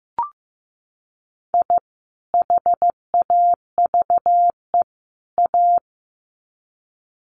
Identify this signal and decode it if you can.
Morse code, heard as a keyed tone, about 15 wpm, 700 Hz. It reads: I HAVE A
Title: Sets of 3 Words 15wpm